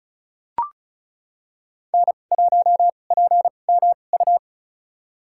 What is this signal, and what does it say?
Morse code, heard as a keyed tone, about 36 wpm, spaced as for 25 wpm, 700 Hz. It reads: N1PMU
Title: USA Call Signs - 2x Character Spacing - 35wpm